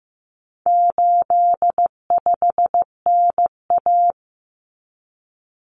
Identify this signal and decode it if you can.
Morse code, heard as a keyed tone, about 15 wpm, 700 Hz. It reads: 85NA
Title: Common Cut Numbers - 4-Digit 15wpm